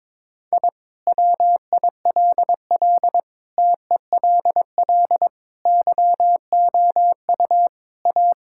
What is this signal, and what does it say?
Morse code, heard as a keyed tone, about 22 wpm, 700 Hz. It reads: I WILL TELL YOU A
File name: practice / 22wpm / Sentences with 2-Word Phrases - 22wpm